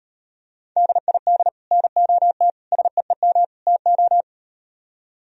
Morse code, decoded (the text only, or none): DID NOT SEEM TO